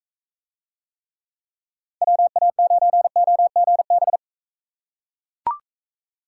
Morse code, heard as a keyed tone, about 42 wpm, 700 Hz. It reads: WA9OGB